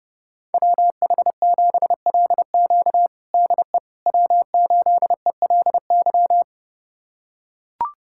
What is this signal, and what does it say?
Morse code, heard as a keyed tone, about 32 wpm, 700 Hz. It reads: WH7LQ DE W8ELY
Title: USA Callsign DE Callsign - 5 Characters - 30wpm